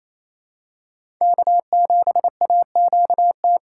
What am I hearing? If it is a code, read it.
Morse code, heard as a keyed tone, about 28 wpm, 700 Hz. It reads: K7AQT
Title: USA Call Signs - 5 Characters - 2x Character Spacing - 28wpm